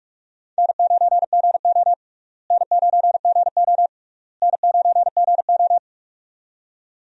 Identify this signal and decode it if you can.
Morse code, heard as a keyed tone, about 46 wpm, 700 Hz. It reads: N9GO N9GO N9GO
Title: USA Call Signs - 4 Characters - Repeated 3x - 45wpm